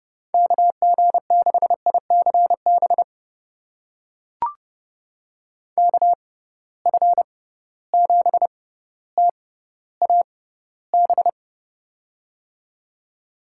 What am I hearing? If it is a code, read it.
Morse code, heard as a keyed tone, about 30 wpm, spaced as for 15 wpm, 700 Hz. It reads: KG6ICB KF7TAB
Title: USA Call Signs - 6 Characters - 6x Character Spacing - 30wpm